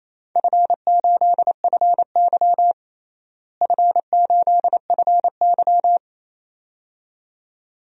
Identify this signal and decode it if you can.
Morse code, heard as a keyed tone, about 28 wpm, 700 Hz. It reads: F8FY F8FY
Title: DXCC-100 Call Signs - 4 Characters - Repeated 2x - 28wpm